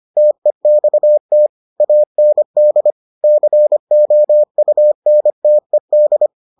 Morse code, heard as a keyed tone, about 25 wpm, 600 Hz. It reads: TEXT AND COUNTED